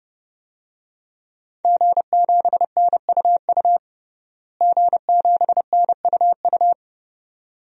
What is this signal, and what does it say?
Morse code, heard as a keyed tone, about 30 wpm, 700 Hz. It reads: G7NUU G7NUU
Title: DXCC-100 Call Signs - 5 Characters - Repeated 2x - 30wpm